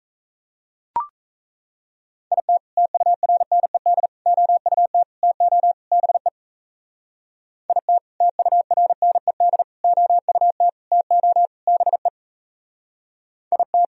Morse code, heard as a keyed tone, about 36 wpm, 700 Hz. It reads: IT TURNED OUT TO BE IT TURNED OUT TO BE IT